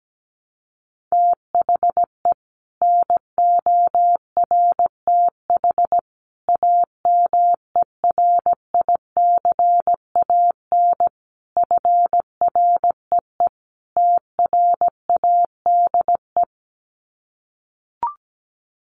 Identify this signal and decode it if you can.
Morse code, heard as a keyed tone, about 17 wpm, 700 Hz. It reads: THE NORTH AMERICAN FREE TRADE
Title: Sets of 5 Words Encore 17wpm